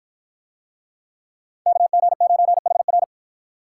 Morse code, heard as a keyed tone, about 53 wpm, 700 Hz. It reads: KG9HR